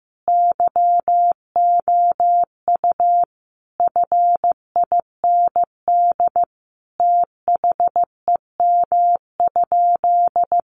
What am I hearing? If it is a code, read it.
Morse code, heard as a keyed tone, about 15 wpm, 700 Hz. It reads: YOU FIND THEM?
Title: Sentences from Top 200 Words 15wpm